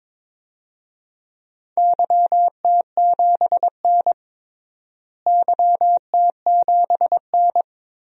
Morse code, heard as a keyed tone, about 21 wpm, 700 Hz. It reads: YT7N YT7N
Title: DXCC-100 Call Signs - 4 Characters - Repeated 3x - 22wpm